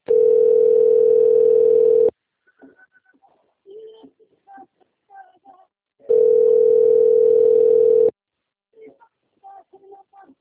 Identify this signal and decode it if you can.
Ringback tone (North America), 440 + 480 Hz.
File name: rg-1600-unknown- - - .933